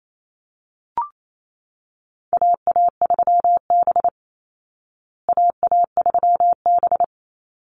Morse code, heard as a keyed tone, about 28 wpm, 700 Hz. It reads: AA3B AA3B
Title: Call Signs Repeated 3x - 28wpm